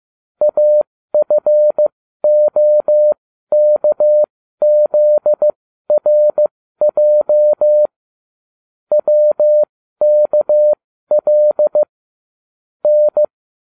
Morse code, heard as a keyed tone, about 15 wpm, spaced as for 12 wpm, 600 Hz. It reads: AFOKZRJ WKL N